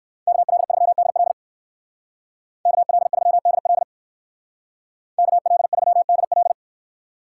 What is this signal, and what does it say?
Morse code, heard as a keyed tone, about 58 wpm, 700 Hz. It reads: KL3DL KL3DL KL3DL